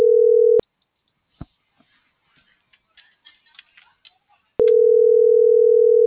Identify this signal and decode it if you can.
Ringback tone (North America), 440 + 480 Hz.
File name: CENTRAL TERRESTRE